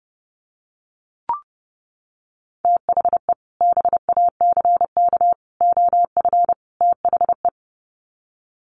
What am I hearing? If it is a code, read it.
Morse code, heard as a keyed tone, about 30 wpm, 700 Hz. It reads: THE BACK OF THE
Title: Sets of 4 Words 30wpm